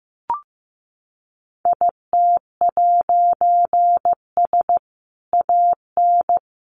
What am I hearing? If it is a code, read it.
Morse code, heard as a keyed tone, about 15 wpm, 700 Hz. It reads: IT'S AN
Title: A Sound Of Thunder 15wpm